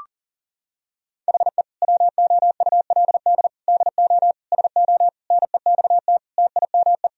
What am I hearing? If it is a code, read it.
Morse code, heard as a keyed tone, about 40 wpm, 700 Hz. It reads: HE WOULD DO SO NEXT TIME
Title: Sentences from Top 1000 Words - Review Part 1 - 40wpm